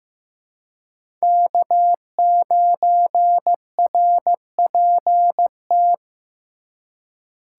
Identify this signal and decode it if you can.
Morse code, heard as a keyed tone, about 15 wpm, 700 Hz. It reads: K9RPT
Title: USA Call Signs - 5 Characters - 3x Character Spacing - 15wpm